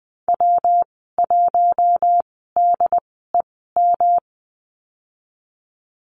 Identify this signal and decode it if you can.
Morse code, heard as a keyed tone, about 20 wpm, spaced as for 15 wpm, 700 Hz. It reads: W1DEM
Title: 219 Lesson 35g - Number 0 - Call Signs 20wpm